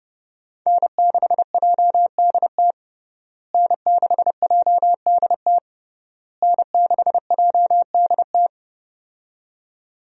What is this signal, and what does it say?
Morse code, heard as a keyed tone, about 30 wpm, 700 Hz. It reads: N6JDT N6JDT N6JDT
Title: USA Call Signs - 5 Characters - Repeated 3x - 30wpm